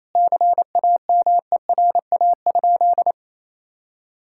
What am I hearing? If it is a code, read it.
Morse code, heard as a keyed tone, about 28 wpm, 700 Hz. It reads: CAMERA?
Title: Sentences from Top 1000 Words - Review Part 5 - 28wpm